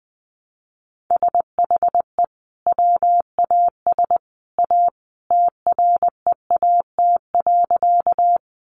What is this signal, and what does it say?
Morse code, heard as a keyed tone, about 20 wpm, 700 Hz. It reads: SHE WAS A TREAT.
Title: Sentences from Top 1300 Words 20wpm